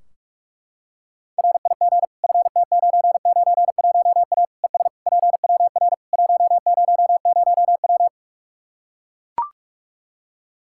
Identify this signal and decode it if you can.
Morse code, heard as a keyed tone, about 42 wpm, 700 Hz. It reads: RIG FT991A ES PWR 100W
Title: Practice QSO - Exchange 2 - Info - 45wpm